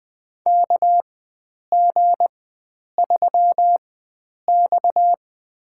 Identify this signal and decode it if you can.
Morse code, heard as a keyed tone, about 20 wpm, spaced as for 9 wpm, 700 Hz. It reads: KG3X
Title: USA Call Signs - 4 Characters - 4x Character Spacing - 20wpm